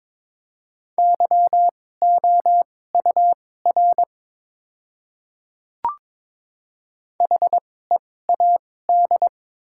Morse code, heard as a keyed tone, about 22 wpm, spaced as for 16 wpm, 700 Hz. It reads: YOUR HEAD
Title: 2x Character Spacing - Top 300 Words - 22wpm